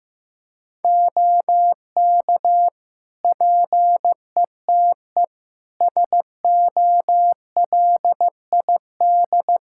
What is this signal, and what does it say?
Morse code, heard as a keyed tone, about 15 wpm, 700 Hz. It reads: OK PETE SOLID
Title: Practice QSO - Exchange 3 - More Info - 15wpm